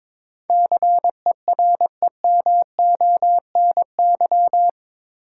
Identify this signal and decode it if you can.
Morse code, heard as a keyed tone, about 22 wpm, 700 Hz. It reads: CEREMONY
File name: practice / 22wpm / Eight Letter Words - 22wpm